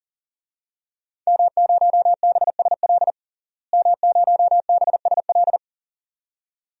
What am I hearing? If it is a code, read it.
Morse code, heard as a keyed tone, about 40 wpm, 700 Hz. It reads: M0BSL M0BSL
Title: DXCC-100 Call Signs - 5 Characters - Repeated 2x - 40wpm